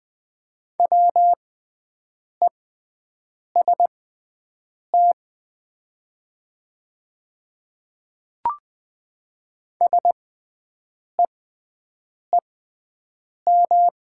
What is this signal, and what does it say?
Morse code, heard as a keyed tone, about 20 wpm, spaced as for 7 wpm, 700 Hz. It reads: WEST SEEM
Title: 6x Character Spacing - Top 500 Words - 20wpm